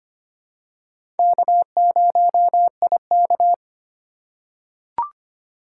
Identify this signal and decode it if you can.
Morse code, heard as a keyed tone, about 25 wpm, 700 Hz. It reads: K0IK
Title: USA Call Signs - 4 Characters - 2x Character Spacing - 25wpm